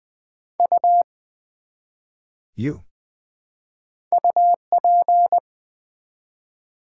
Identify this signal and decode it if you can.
Morse code, heard as a keyed tone, about 20 wpm, 700 Hz. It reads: U UP
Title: Build A Word - 1 Letter At A Time - 5-Letter Words - 20wpm